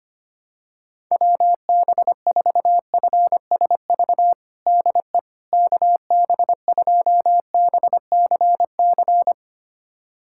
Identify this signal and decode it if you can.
Morse code, heard as a keyed tone, about 25 wpm, 700 Hz. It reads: WB4FSV DE KB2BCC
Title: USA Callsign DE Callsign - 6 Characters - 25wpm